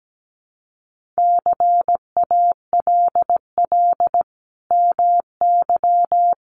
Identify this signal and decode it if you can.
Morse code, heard as a keyed tone, about 17 wpm, 700 Hz. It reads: CALL MY
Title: Sentences from Top 1500 Words 17wpm